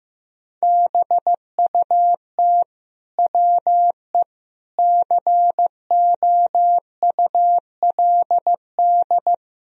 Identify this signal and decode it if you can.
Morse code, heard as a keyed tone, about 15 wpm, 700 Hz. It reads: BUT WE COULD
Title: Sentences from Top 1000 Words - Review Part 2 - 15wpm